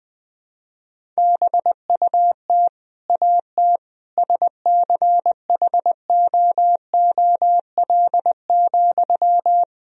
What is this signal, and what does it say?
Morse code, heard as a keyed tone, about 20 wpm, 700 Hz. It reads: BUT AT SCHOOL,
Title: Sentences from Top 1000 Words - Review Part 3 - 20wpm